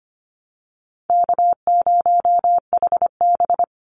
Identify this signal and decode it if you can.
Morse code, heard as a keyed tone, about 25 wpm, 700 Hz. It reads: K0HB
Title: Call Signs - 25wpm